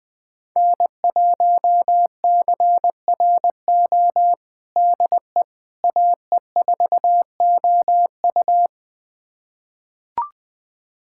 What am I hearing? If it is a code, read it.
Morse code, heard as a keyed tone, about 20 wpm, 700 Hz. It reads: N1CRO DE AE4OU
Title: USA Callsign DE Callsign - 5 Characters - 20wpm